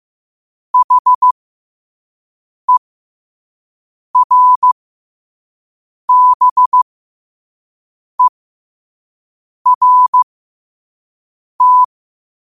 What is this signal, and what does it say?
Morse code, heard as a keyed tone, about 15 wpm, spaced as for 5 wpm, 1000 Hz. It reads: HERBERT
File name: qotdfiles / 5wpm